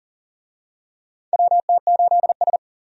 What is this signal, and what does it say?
Morse code, heard as a keyed tone, about 40 wpm, 700 Hz. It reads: WT8S
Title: USA Call Signs - 4 Characters - 40wpm